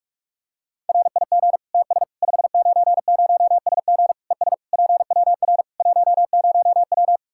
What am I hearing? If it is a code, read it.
Morse code, heard as a keyed tone, about 46 wpm, 700 Hz. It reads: RIG TS 590SG ES PWR 10W
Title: Practice QSO - Exchange 2 - Info - 45wpm